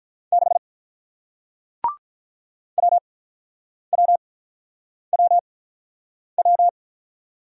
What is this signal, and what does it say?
Morse code, heard as a keyed tone, about 48 wpm, 700 Hz. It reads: 6 W W W W